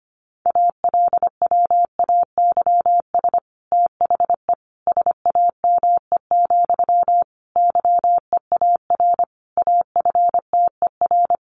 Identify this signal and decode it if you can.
Morse code, heard as a keyed tone, about 25 wpm, 700 Hz. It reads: ALWAYS THE SAME, YEAR AFTER
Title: Sentences from Top 300 Words 25wpm